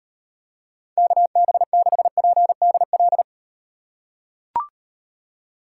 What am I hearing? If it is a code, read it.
Morse code, heard as a keyed tone, about 38 wpm, 700 Hz. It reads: KB6PDL